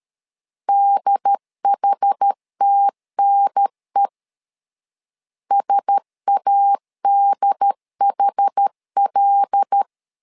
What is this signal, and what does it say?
Morse code, heard as a keyed tone, about 13 wpm, 800 Hz. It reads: DHTNE SADHL
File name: LEZIONI / 5 LBVC / 5 L 12wpm 1913